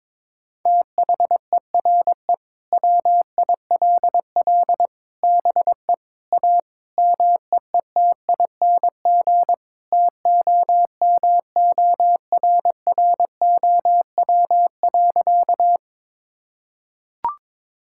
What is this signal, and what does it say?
Morse code, heard as a keyed tone, about 22 wpm, 700 Hz. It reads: THERE WILL BE A MEETING TOMORROW.
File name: practice / 22wpm / Sentences with 4-Word Phrases - 22wpm